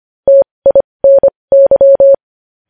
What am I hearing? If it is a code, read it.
Morse code, heard as a keyed tone, about 25 wpm, spaced as for 20 wpm, 550 Hz. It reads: TINY
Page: news headlines in morse code at 20 wpm